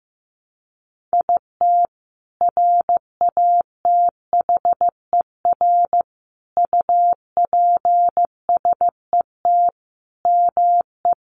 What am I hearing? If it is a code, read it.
Morse code, heard as a keyed tone, about 15 wpm, 700 Hz. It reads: IT RATHER UPSET ME
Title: Sentences from Top 900 Words 15wpm